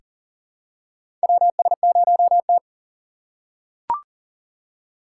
Morse code, heard as a keyed tone, about 40 wpm, 700 Hz. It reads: WS0T
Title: USA Call Signs - 4 Characters - 2x Character Spacing - 40wpm